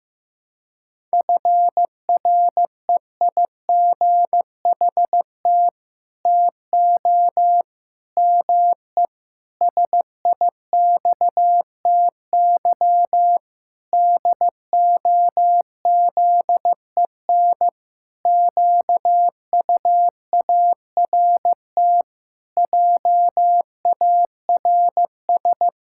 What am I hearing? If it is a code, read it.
Morse code, heard as a keyed tone, about 15 wpm, 700 Hz. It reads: FREIGHT TO ME SIXTY DOZEN QUART JARS
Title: Pangram Challenge v2 - 15wpm